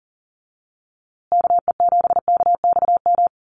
Morse code, heard as a keyed tone, about 40 wpm, 700 Hz. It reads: KE7KXM
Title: USA Call Signs Repeated 2x 40wpm